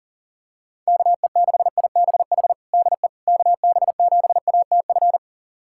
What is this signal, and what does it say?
Morse code, heard as a keyed tone, about 40 wpm, 690 Hz. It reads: KE6IBH DE KB7ATF
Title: USA Callsign DE Callsign - 6 Characters - 40wpm